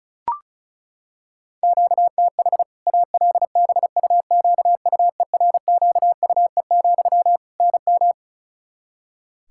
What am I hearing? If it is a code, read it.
Morse code, heard as a keyed tone, about 34 wpm, 700 Hz. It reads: QTH ALBUQUERQUE, NM
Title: Practice QSO - Exchange 1 - Beginning - 35wpm